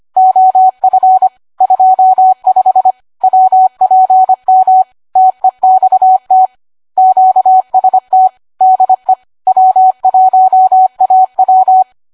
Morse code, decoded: OF 25 WPM TEXT QST DE W1AW